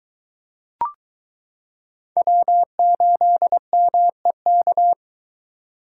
Morse code, heard as a keyed tone, about 22 wpm, 700 Hz. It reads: W8MEK